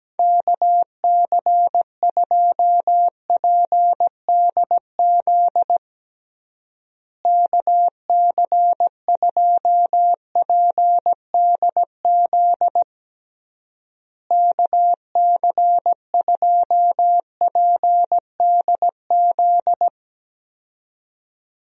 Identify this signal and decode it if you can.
Morse code, heard as a keyed tone, about 17 wpm, 700 Hz. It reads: KC2PDZ KC2PDZ KC2PDZ
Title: USA Call Signs - 6 Characters - Repeated 3x - 17wpm